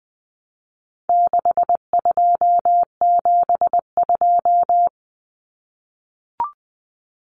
Morse code, heard as a keyed tone, about 20 wpm, 700 Hz. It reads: 6272
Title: Numbers - 4-Digit 20wpm